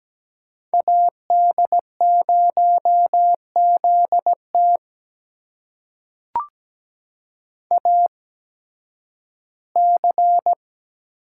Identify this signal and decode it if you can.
Morse code, heard as a keyed tone, about 17 wpm, spaced as for 11 wpm, 700 Hz. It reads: AD0ZT AC
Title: USA Call Signs - 5 Characters - 8x Character Spacing - 17wpm